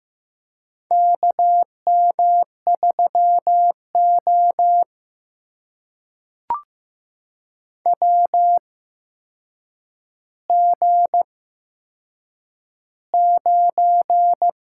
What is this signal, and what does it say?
Morse code, heard as a keyed tone, about 15 wpm, spaced as for 7 wpm, 700 Hz. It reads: KM3O WG9
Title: USA Call Signs - 4 Characters - 8x Character Spacing - 15wpm